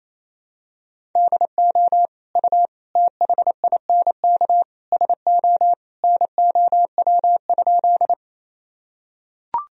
Morse code, heard as a keyed tone, about 28 wpm, 700 Hz. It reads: DO U THINK SO NOW?
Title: Sentences from Top 200 Words and Abbreviations 28wpm